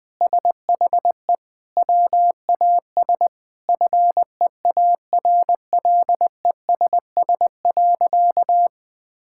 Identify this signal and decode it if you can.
Morse code, heard as a keyed tone, about 20 wpm, 700 Hz. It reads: SHE WAS FEARLESS.
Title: Sentences with 4-Word Phrases - 20wpm